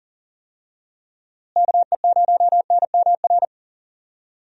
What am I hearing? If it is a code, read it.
Morse code, heard as a keyed tone, about 38 wpm, 700 Hz. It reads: KE0NMR
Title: USA Call Signs - 6 Characters - Repeated 2x - 40wpm